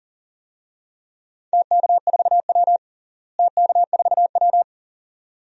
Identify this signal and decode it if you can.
Morse code, heard as a keyed tone, about 38 wpm, 700 Hz. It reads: TK4W TK4W
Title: DXCC-100 Call Signs - 4 Characters - Repeated 2x - 40wpm